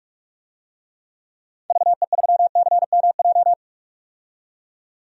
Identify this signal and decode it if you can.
Morse code, heard as a keyed tone, about 45 wpm, 700 Hz. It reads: VE3CMJ
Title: DXCC-100 Call Signs - 6 Characters - 45wpm